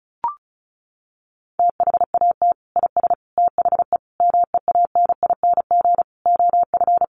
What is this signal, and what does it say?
Morse code, heard as a keyed tone, about 34 wpm, 700 Hz. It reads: THAT IS THE MEANING OF
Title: Sentences from Top 900 Words 35wpm